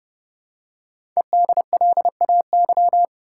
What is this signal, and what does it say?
Morse code, heard as a keyed tone, about 29 wpm, 700 Hz. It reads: EDLAY
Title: Code Groups - 5 Letters - 30wpm